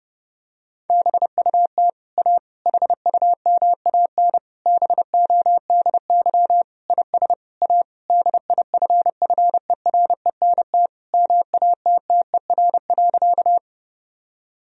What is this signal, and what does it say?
Morse code, heard as a keyed tone, about 30 wpm, 700 Hz. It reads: BUT A HUMAN BODY IS A DIFFERENT MATTER.
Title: Sentences from Top 1000 Words - Review Part 3 - 30wpm